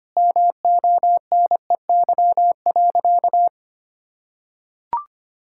Morse code, heard as a keyed tone, about 25 wpm, 700 Hz. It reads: MONEY.
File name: practice / 25wpm / Difficult Sentences with Hints - 25wpm